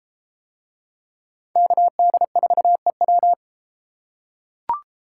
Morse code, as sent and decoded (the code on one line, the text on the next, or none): -.- -.. ....- . .--
KD4EW